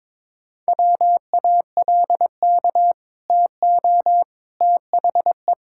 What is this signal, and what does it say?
Morse code, heard as a keyed tone, about 22 wpm, 700 Hz. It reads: WALK TO THE
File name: practice / 22wpm / Sentences with 4-Word Phrases - 22wpm